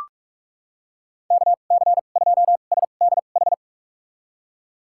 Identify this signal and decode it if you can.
Morse code, heard as a keyed tone, about 42 wpm, spaced as for 31 wpm, 700 Hz. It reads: KC2SDH
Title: USA Call Signs - 6 Characters - 2x Character Spacing - 45wpm